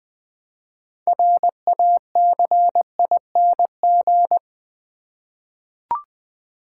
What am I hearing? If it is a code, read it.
Morse code, heard as a keyed tone, about 20 wpm, 700 Hz. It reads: RACING
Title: 6 Letter Words - First 3 Letters Given - 20wpm